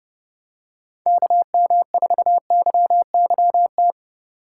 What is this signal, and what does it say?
Morse code, heard as a keyed tone, about 29 wpm, 700 Hz. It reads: KM4YYT